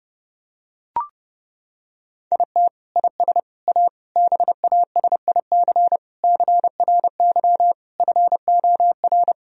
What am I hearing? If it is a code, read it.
Morse code, heard as a keyed tone, about 30 wpm, 700 Hz. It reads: IT IS A BASIC CRY FOR
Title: Sentences from Top 1000 Words - Review Part 3 - 30wpm